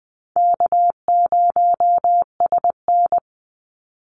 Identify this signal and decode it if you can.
Morse code, heard as a keyed tone, about 20 wpm, 700 Hz. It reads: K0SN
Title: Call Signs Repeated 3x - 20wpm